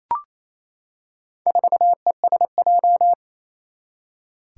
Morse code, heard as a keyed tone, about 28 wpm, 700 Hz. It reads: 4ESJ
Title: Four Challenging Letter-Number Combinations 28wpm